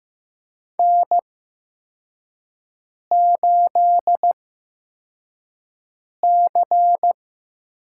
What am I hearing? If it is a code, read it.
Morse code, heard as a keyed tone, about 15 wpm, spaced as for 4 wpm, 700 Hz. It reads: N8C
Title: USA Call Signs - 5 Characters - 8x Character Spacing - 15wpm